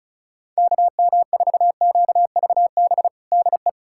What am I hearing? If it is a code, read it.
Morse code, heard as a keyed tone, about 36 wpm, 700 Hz. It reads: KM4QVB DE
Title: USA Callsign DE Callsign - 6 Characters - 35wpm